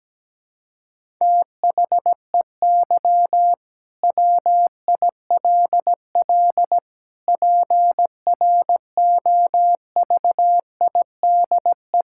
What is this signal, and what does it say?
Morse code, heard as a keyed tone, about 17 wpm, 700 Hz. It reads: THEY WILL PROVIDE
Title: Sentences from Top 1000 Words - Review Part 5 - 17wpm